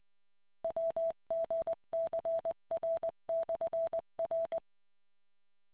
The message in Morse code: WGCR/R

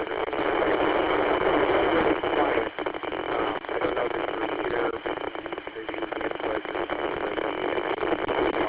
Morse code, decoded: none

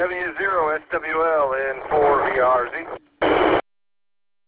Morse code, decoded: none

